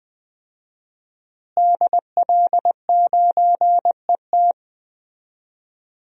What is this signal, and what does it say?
Morse code, heard as a keyed tone, about 20 wpm, 700 Hz. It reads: DL9ET
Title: DXCC-100 Call Signs - 5 Characters - Repeated 3x - 20wpm